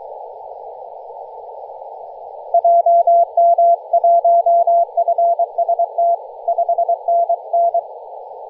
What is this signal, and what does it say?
Morse code, heard as a keyed tone, about 23 wpm, 680 Hz. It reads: JM1FST 5NN
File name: posts / 2016-nov-13-0348z-t30jy-28-034mhz-cw